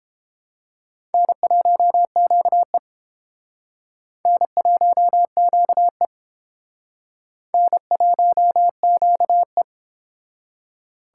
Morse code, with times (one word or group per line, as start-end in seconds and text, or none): 1.1-2.8 N1QE
4.3-6.1 N1QE
7.5-9.6 N1QE